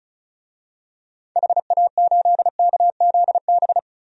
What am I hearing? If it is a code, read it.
Morse code, heard as a keyed tone, about 34 wpm, 700 Hz. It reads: HA8KZB
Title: DXCC-100 Call Signs - 6 Characters - 35wpm